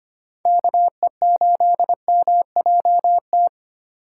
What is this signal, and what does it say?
Morse code, heard as a keyed tone, about 25 wpm, 700 Hz. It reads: KE8MJT
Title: USA Call Signs - 6 Characters - 4x Character Spacing - 25wpm